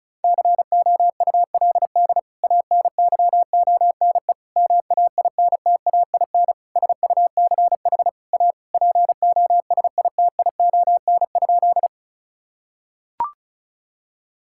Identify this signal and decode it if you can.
Morse code, heard as a keyed tone, about 36 wpm, 700 Hz. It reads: COULD ANYONE MAINTAIN SUCH A POSITION?
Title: Sentences from Top 1000 Words - Review Part 5 - 35wpm